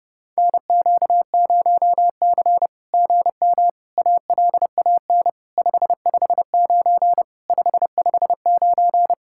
Morse code, heard as a keyed tone, about 28 wpm, 700 Hz. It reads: NQ0C GM ALAN 559 559
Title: SOTA QSO Practice 30wpm